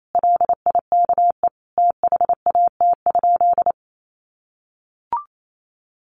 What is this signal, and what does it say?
Morse code, heard as a keyed tone, about 28 wpm, 700 Hz. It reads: LIKE THAT?
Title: Sentences from Top 1400 Words 28wpm